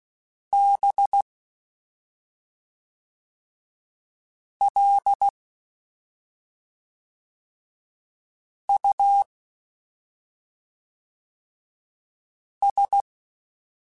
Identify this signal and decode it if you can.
Morse code, heard as a keyed tone, about 16 wpm, 800 Hz. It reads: B L U S
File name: Special INGs / Special INGs-mp3s / SlowToFaster ing @5